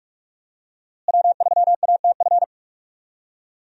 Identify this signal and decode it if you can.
Morse code, heard as a keyed tone, about 42 wpm, 700 Hz. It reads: W3ATF
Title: USA Call Signs - 5 Characters - Repeated 2x - 45wpm